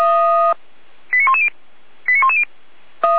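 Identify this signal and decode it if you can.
Five-tone selective call (ZVEI), heard as quick runs of five tones, tones of about 70 ms. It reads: 89110 89110
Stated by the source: ZVEI five-tone call received on an SDR.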